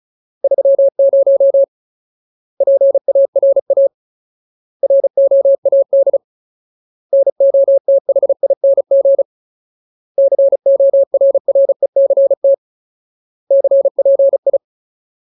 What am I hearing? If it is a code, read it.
Morse code, heard as a keyed tone, about 34 wpm, 550 Hz. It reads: 30 PARA ROAD NOTHING CORRECT CPI
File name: Ultimate Word List / 35WPM Ultimate